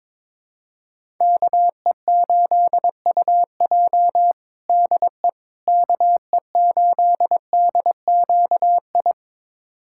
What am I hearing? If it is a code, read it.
Morse code, heard as a keyed tone, about 22 wpm, 700 Hz. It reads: KE8UJ DE KE8DQI
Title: USA Callsign DE Callsign 22wpm